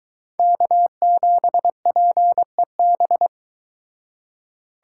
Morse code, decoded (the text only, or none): K7PEB